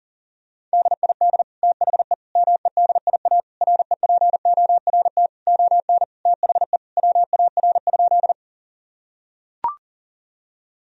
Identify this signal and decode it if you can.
Morse code, heard as a keyed tone, about 40 wpm, 700 Hz. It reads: DID THE MEDIA REPORT ON THE WAR?
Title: Sentences from Top 1000 Words - Review Part 4 - 40wpm